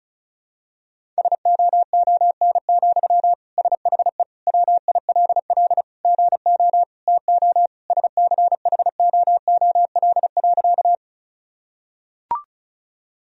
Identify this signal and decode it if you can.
Morse code, heard as a keyed tone, about 34 wpm, 700 Hz. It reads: SOON, SHE WILL GO TO SCHOOL.
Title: Sentences from Top 1000 Words - Review Part 1 - 35wpm